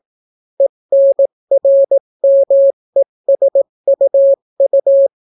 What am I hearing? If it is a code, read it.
Morse code, heard as a keyed tone, about 18 wpm, 550 Hz. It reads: ENRMESUU